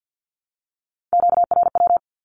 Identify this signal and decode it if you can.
Morse code, heard as a keyed tone, about 50 wpm, 700 Hz. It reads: QRW
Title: Q-Codes Repeated 3x 50wpm